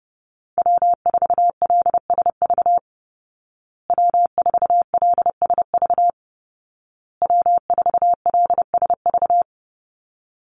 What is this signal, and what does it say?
Morse code, heard as a keyed tone, about 30 wpm, 700 Hz. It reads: W4LSV W4LSV W4LSV
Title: Call Signs Repeated 2x - 30wpm